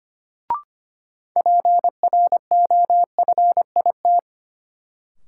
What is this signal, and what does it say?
Morse code, heard as a keyed tone, about 25 wpm, 700 Hz. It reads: PROFIT
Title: Sentences from Top 800 Words - New Word Drill - 25wpm